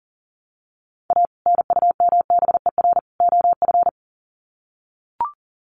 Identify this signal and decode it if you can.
Morse code, heard as a keyed tone, about 40 wpm, 700 Hz. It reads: A NUMBER OF
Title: Sets of 3 Words 40wpm